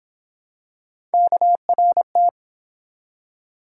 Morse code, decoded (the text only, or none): KRT